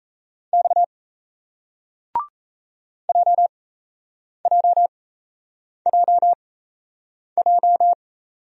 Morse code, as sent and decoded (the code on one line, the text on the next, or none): -..- / .--- .--- .--- .---
X JJJJ